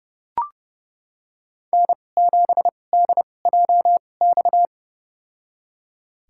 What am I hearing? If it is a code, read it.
Morse code, heard as a keyed tone, about 30 wpm, spaced as for 22 wpm, 700 Hz. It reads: N7DJX
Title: USA Call Signs - 5 Characters - 2x Character Spacing - 30wpm